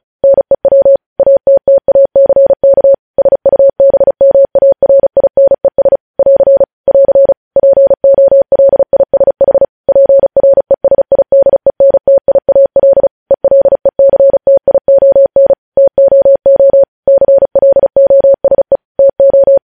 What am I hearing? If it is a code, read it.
Morse code, heard as a keyed tone, about 36 wpm, 550 Hz. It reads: NEW ATTACK SUBMARINES <AR> <AR> POLISH PRESIDENTIAL ELECTION TOO CLOSE TO
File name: podcast / News headlines-1x-35x35